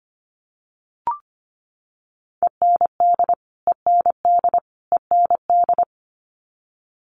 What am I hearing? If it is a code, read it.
Morse code, heard as a keyed tone, about 25 wpm, 700 Hz. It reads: END END END
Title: Top 300 Words Repeated 25wpm